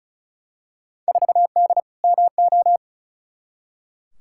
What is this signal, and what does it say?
Morse code, heard as a keyed tone, about 36 wpm, 700 Hz. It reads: 4D MO
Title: Field Day Exchanges Encore - 35wpm